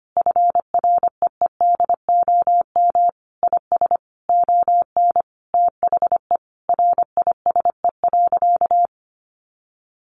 Morse code, heard as a keyed tone, about 25 wpm, 700 Hz. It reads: FREEDOM IS ON THE RISE.
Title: Sentences from Top 1300 Words 25wpm